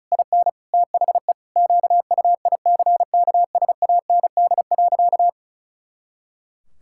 Morse code, heard as a keyed tone, about 34 wpm, 700 Hz. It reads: IN THE QUICKSAND.
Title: Pangram Challenge v2 - 35wpm